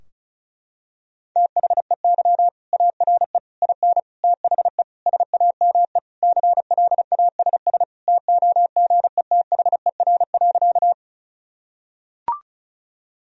Morse code, heard as a keyed tone, about 36 wpm, 700 Hz. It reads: THEY ARE IN THE SAME CLASS TOGETHER.
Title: Sentences from Top 1000 Words - Review Part 2 - 35wpm